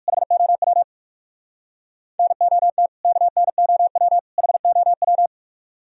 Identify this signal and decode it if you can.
Morse code, heard as a keyed tone, about 60 wpm, 700 Hz. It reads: HOW NOT KNOW HOW